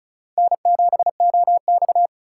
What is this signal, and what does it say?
Morse code, heard as a keyed tone, about 36 wpm, 700 Hz. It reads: N7OX
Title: USA Call Signs - 4x Character Spacing - 35wpm